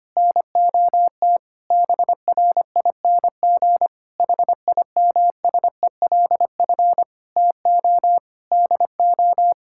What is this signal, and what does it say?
Morse code, heard as a keyed tone, about 25 wpm, 700 Hz. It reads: NOT BRING HIMSELF TO DO
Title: Sentences from Top 1600 Words 25wpm